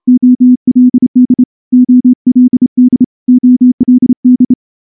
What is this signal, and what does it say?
Morse code, heard as a keyed tone, about 28 wpm, 250 Hz. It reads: OLD OLD OLD
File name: morse-training-30WPM